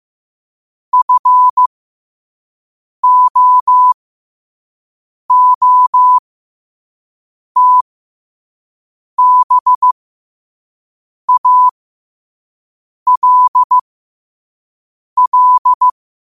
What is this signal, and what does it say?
Morse code, heard as a keyed tone, about 15 wpm, spaced as for 5 wpm, 1000 Hz. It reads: FOOTBALL
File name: qotdfiles / 5wpm